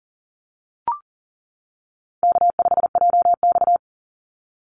Morse code, heard as a keyed tone, about 40 wpm, 700 Hz. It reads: K5JX
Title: Easy Call Signs - 40wpm